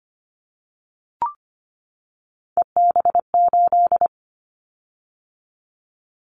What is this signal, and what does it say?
Morse code, heard as a keyed tone, about 25 wpm, 700 Hz. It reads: EB8
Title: Callsign Prefixes for Top 50 DXCC - 25wpm